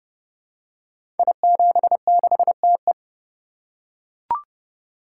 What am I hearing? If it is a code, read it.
Morse code, heard as a keyed tone, about 28 wpm, 700 Hz. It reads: I76TE
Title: Code Groups - 5 Letter:Numbers - 30wpm